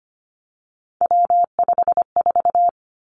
Morse code, heard as a keyed tone, about 25 wpm, 700 Hz. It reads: W54
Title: Three Challenging Letter-Number Combinations 25wpm